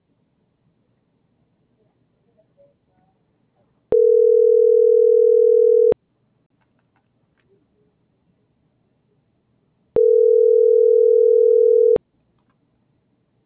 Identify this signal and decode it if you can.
Ringback tone (North America), 440 + 480 Hz.